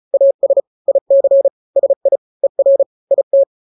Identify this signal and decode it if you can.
Morse code, heard as a keyed tone, about 34 wpm, 550 Hz. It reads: AS IC SI ER IT
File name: n-grams / 2-grams 35 30 WPM